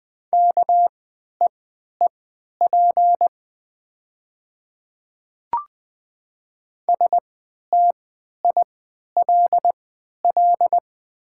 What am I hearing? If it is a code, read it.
Morse code, heard as a keyed tone, about 20 wpm, spaced as for 11 wpm, 700 Hz. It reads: KEEP STILL
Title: 3x Character Spacing - Top 300 Words - 20wpm